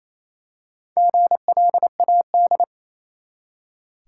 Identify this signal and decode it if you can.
Morse code, heard as a keyed tone, about 28 wpm, 700 Hz. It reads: GLAD
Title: Four Letter Words - 28wpm